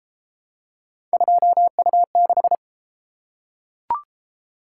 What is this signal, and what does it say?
Morse code, heard as a keyed tone, about 30 wpm, 700 Hz. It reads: 2U6